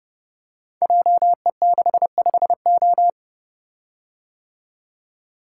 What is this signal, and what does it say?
Morse code, heard as a keyed tone, about 30 wpm, 700 Hz. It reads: JE65O
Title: Code Groups - 5 Letter:Numbers - 30wpm